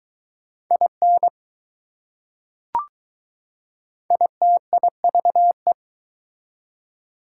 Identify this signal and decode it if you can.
Morse code, heard as a keyed tone, about 21 wpm, 690 Hz. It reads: IN ITIVE